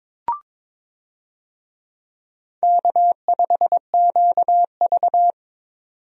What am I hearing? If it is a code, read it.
Morse code, heard as a keyed tone, about 21 wpm, 700 Hz. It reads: K5QV
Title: USA Call Signs - 4 Characters - 22wpm